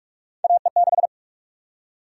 Morse code, heard as a keyed tone, about 43 wpm, 700 Hz. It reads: AE6
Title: USA Call Sign Prefixes - Repeated 2x - 45wpm